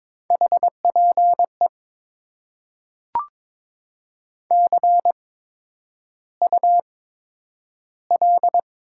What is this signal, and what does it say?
Morse code, heard as a keyed tone, about 22 wpm, spaced as for 9 wpm, 690 Hz. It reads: HPE CUL
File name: Top 90 QSO Elements - 8x Character Spacing - 22wpm